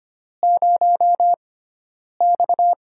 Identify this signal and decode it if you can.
Morse code, heard as a keyed tone, about 25 wpm, spaced as for 9 wpm, 700 Hz. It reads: 0X
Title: USA Call Signs - 4 Characters - 6x Character Spacing - 25wpm